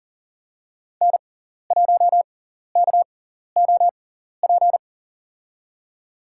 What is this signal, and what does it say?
Morse code, heard as a keyed tone, about 40 wpm, spaced as for 14 wpm, 700 Hz. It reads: N1KOP
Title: USA Call Signs - 5 Characters - 6x Character Spacing - 40wpm